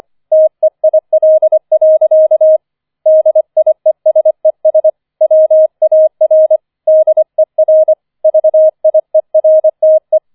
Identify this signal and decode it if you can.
Morse code, heard as a keyed tone, about 25 wpm, 630 Hz. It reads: TEIL. DIESES WAR DER VIERTE
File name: 004 Vierter Streich 25wpm